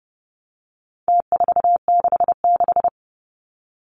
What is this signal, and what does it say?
Morse code, heard as a keyed tone, about 30 wpm, 700 Hz. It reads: T466
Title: Numbers - Contest Style - 4-Digit 30wpm